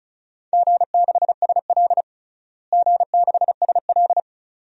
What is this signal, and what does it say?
Morse code, heard as a keyed tone, about 36 wpm, 700 Hz. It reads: G6SL G6SL
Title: DXCC-100 Call Signs - 4 Characters - Repeated 2x - 35wpm